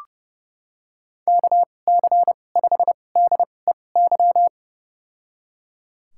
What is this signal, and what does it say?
Morse code, heard as a keyed tone, about 30 wpm, spaced as for 22 wpm, 700 Hz. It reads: KC5DEY
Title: USA Call Signs - 6 Characters - 2x Character Spacing - 30wpm